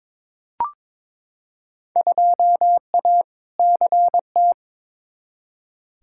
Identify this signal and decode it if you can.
Morse code, heard as a keyed tone, about 22 wpm, 700 Hz. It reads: 2A CT
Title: Field Day Exchanges - 22wpm